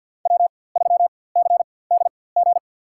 Morse code, heard as a keyed tone, about 47 wpm, spaced as for 23 wpm, 700 Hz. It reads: W3CDG
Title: USA Call Signs - 5 Characters - 4x Character Spacing - 50wpm